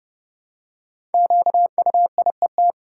Morse code, heard as a keyed tone, about 32 wpm, 700 Hz. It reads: QUIET
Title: Five Letter Words - 30wpm